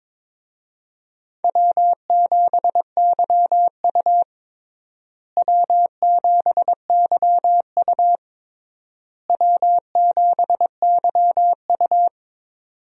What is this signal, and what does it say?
Morse code, heard as a keyed tone, about 22 wpm, 700 Hz. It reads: W7YU W7YU W7YU
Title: USA Call Signs - 4 Characters - Repeated 3x - 22wpm